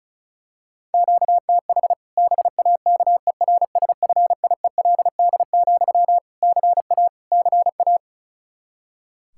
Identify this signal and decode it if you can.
Morse code, heard as a keyed tone, about 36 wpm, 700 Hz. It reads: QTH BAKERSFIELD, CA CA
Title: Practice QSO - Exchange 1 - Beginning - 35wpm